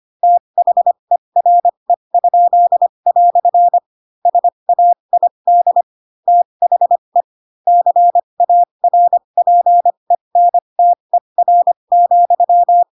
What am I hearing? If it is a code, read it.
Morse code, heard as a keyed tone, about 25 wpm, spaced as for 22 wpm, 700 Hz. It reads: THERE?" SAID THE CARPENTER,
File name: Pinocchio / Pinocchio ch2 18 25 ik1hge